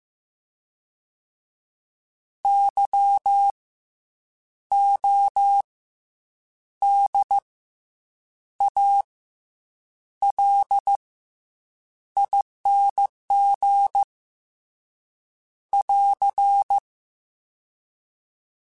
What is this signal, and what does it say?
Morse code, heard as a keyed tone, about 15 wpm, spaced as for 7 wpm, 800 Hz. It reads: YODALING <AR>